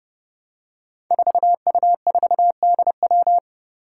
Morse code, heard as a keyed tone, about 30 wpm, 700 Hz. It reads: 4U4DW
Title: Five Challenging Letter-Number Combinations 30wpm